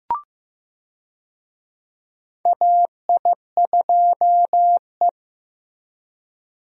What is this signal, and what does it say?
Morse code, heard as a keyed tone, about 15 wpm, 700 Hz. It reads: AI2E
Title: USA Call Signs - 4 Characters - Repeated 3x - 15wpm